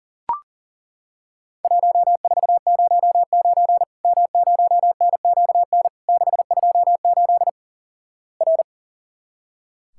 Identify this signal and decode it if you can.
Morse code, heard as a keyed tone, about 38 wpm, 700 Hz. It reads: 1409 M0NQN 628 R
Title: Worked All Europe DX Contest - QTC Exchange - 40wpm